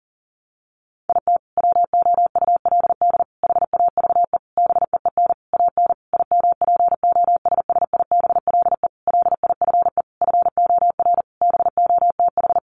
Morse code, heard as a keyed tone, about 37 wpm, 700 Hz. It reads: IT WOULD HAVE BEEN AN IMPOSSIBLE LIFE FOR BOTH
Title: Sentences from Top 1100 Words 40wpm